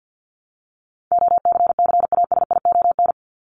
Morse code, heard as a keyed tone, about 51 wpm, 700 Hz. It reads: OCCASION
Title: Top 1501-1750 Words 50wpm